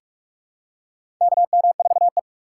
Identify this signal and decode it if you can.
Morse code, heard as a keyed tone, about 46 wpm, 700 Hz. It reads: KM4E
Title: USA Call Signs - 4 Characters - 2x Character Spacing - 45wpm